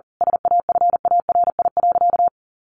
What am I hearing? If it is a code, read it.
Morse code, heard as a keyed tone, about 40 wpm, 700 Hz. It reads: SAFARI.
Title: A Sound Of Thunder 40wpm